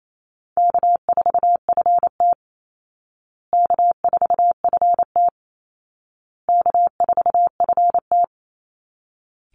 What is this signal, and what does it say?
Morse code, heard as a keyed tone, about 28 wpm, 700 Hz. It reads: K4FT K4FT K4FT
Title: Call Signs Repeated 2x - 28wpm